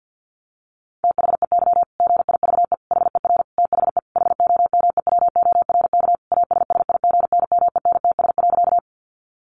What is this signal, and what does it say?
Morse code, heard as a keyed tone, about 50 wpm, 700 Hz. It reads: THEY GIVE HER THE HOMEWORK ASSIGNMENTS.
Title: Easy Phrases 50wpm